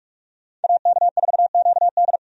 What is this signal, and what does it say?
Morse code, heard as a keyed tone, about 46 wpm, 700 Hz. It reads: AK4QD
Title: USA Call Signs - 5 Characters - 45wpm